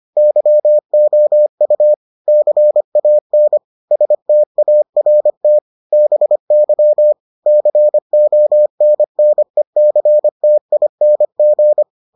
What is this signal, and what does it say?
Morse code, heard as a keyed tone, about 25 wpm, 600 Hz. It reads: YOU CAN START BY CONNECTING